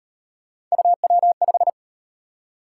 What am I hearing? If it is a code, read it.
Morse code, heard as a keyed tone, about 38 wpm, 700 Hz. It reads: UW5